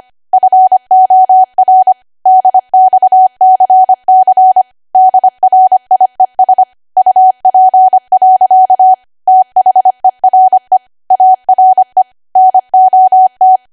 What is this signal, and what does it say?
Morse code, heard as a keyed tone, about 25 wpm, 750 Hz. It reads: FOR DXCC DRIES UP. THERE ARE NOT
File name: cwa / W1AW-Level3 / 25WPM